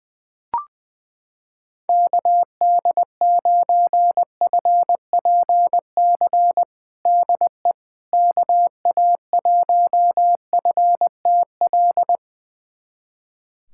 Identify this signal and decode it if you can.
Morse code, heard as a keyed tone, about 20 wpm, 700 Hz. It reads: KD9FPC DE KA1FTL
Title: USA Callsign DE Callsign - 6 Characters - 20wpm